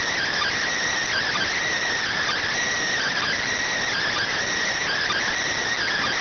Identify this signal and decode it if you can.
Emergency Alert System (SAME) tones (header bursts), heard as harsh buzzing bursts. It is heard from 1.1 to 5.3 s.
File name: findings / SAGITTAL 43mm 4mm.gen-06 rx